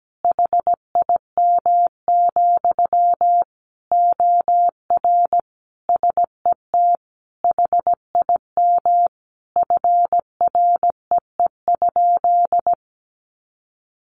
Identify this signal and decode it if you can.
Morse code, heard as a keyed tone, about 17 wpm, 700 Hz. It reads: HIM, OR SET HIM FREE?
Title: Sentences from Top 1100 Words 17wpm